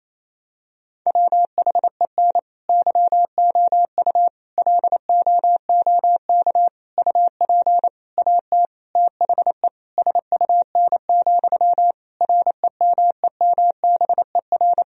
Morse code, heard as a keyed tone, about 28 wpm, 700 Hz. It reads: WHEN YOU LOOK UP AT THE SUN, REMEMBER
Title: Sentences from Top 1000 Words - Review Part 1 - 28wpm